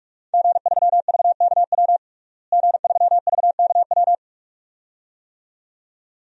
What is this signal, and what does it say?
Morse code, heard as a keyed tone, about 46 wpm, 700 Hz. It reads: G3VKW G3VKW
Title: DXCC-100 Call Signs - 5 Characters - Repeated 2x - 45wpm